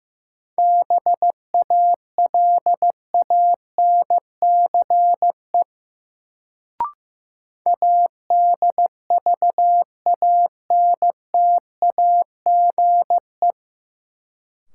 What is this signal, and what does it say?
Morse code, heard as a keyed tone, about 15 wpm, 700 Hz. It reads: BALANCE ADVANTAGE
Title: Sentences from Top 900 Words - New Word Drill - 15wpm